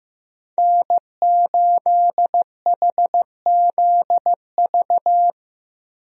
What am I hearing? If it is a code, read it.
Morse code, heard as a keyed tone, about 15 wpm, 700 Hz. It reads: N8HZV
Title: USA Call Signs - 5 Characters - Repeated 3x - 15wpm